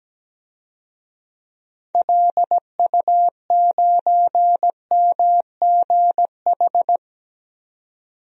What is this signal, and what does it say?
Morse code, heard as a keyed tone, about 17 wpm, 700 Hz. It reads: LU9MGH
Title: DXCC-100 Call Signs - 6 Characters - 17wpm